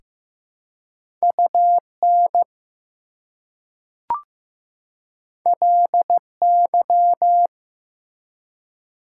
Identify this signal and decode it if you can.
Morse code, heard as a keyed tone, about 15 wpm, 700 Hz. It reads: UN LY
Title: Prefixes and Suffixes 15wpm